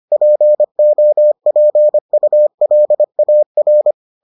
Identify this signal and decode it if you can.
Morse code, heard as a keyed tone, about 25 wpm, 600 Hz. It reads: POPULAR